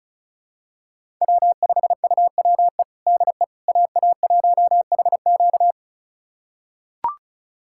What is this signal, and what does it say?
Morse code, heard as a keyed tone, about 36 wpm, 700 Hz. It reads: W5UWE DE AA1HQ
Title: USA Callsign DE Callsign - 5 Characters - 35wpm